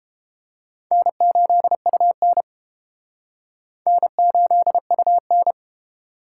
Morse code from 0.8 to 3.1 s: N8UN

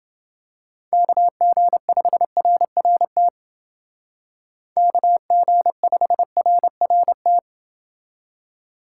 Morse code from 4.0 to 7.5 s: KG5RRT

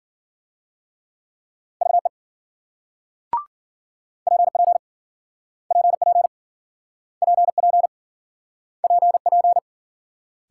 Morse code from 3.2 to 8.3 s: PP PP PP

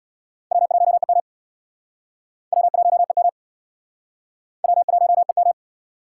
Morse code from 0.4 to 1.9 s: W1EM

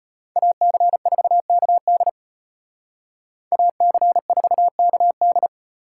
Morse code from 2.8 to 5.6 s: AC4KD

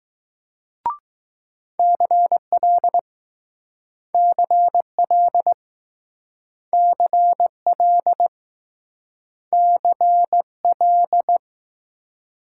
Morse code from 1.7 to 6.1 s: CL CL

